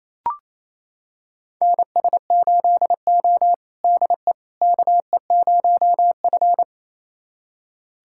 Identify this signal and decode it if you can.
Morse code, heard as a keyed tone, about 28 wpm, 700 Hz. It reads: NS8O DE KE0F
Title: USA Callsign DE Callsign - 4 Characters - 28wpm